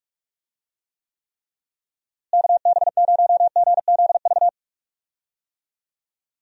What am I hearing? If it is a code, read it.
Morse code, heard as a keyed tone, about 46 wpm, 700 Hz. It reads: KB0GZV
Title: USA Call Signs - 6 Characters - Repeated 3x - 45wpm